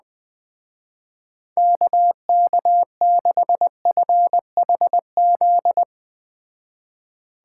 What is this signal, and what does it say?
Morse code, heard as a keyed tone, about 20 wpm, 700 Hz. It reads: KK6FHZ
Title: Speed Racing - USA 2x3 Call Signs 17wpm